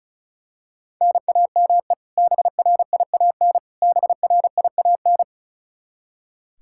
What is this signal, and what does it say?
Morse code, heard as a keyed tone, about 34 wpm, 700 Hz. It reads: NAME BRIAN BRIAN
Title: K1USN SST QSO Practice 35wpm